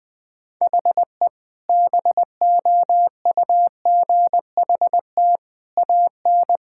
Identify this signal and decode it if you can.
Morse code, heard as a keyed tone, about 20 wpm, 700 Hz. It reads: HE BOUGHT AN
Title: Sentences with 2-Word Phrases - 20wpm